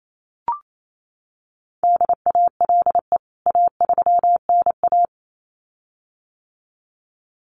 Morse code, heard as a keyed tone, about 28 wpm, 700 Hz. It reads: DALE A3NA
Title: CWT Exchange 28wpm